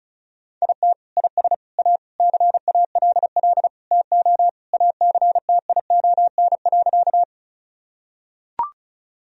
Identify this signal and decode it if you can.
Morse code, heard as a keyed tone, about 36 wpm, 700 Hz. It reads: IT IS A CALL TO ACTION.
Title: Sentences from Top 1000 Words - Review Part 4 - 35wpm